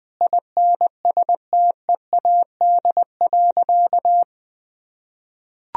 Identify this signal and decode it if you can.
Morse code, heard as a keyed tone, about 20 wpm, 700 Hz. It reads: INSTEAD.
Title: Sentences from Top 1000 Words - Review Part 4 - 20wpm